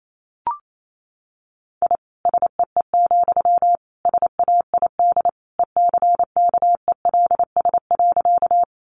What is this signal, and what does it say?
Morse code, heard as a keyed tone, about 28 wpm, 700 Hz. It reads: I SEE, SAID ECKELS.
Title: A Sound Of Thunder 28wpm